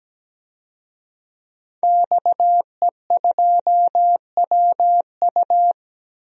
Morse code, heard as a keyed tone, about 17 wpm, 700 Hz. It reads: XE2WU
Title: DXCC-100 Call Signs - 5 Characters - 17wpm